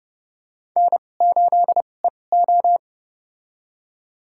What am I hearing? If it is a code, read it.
Morse code, heard as a keyed tone, about 31 wpm, spaced as for 22 wpm, 700 Hz. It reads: N8EO
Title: USA Call Signs - 4 Characters - 2x Character Spacing - 30wpm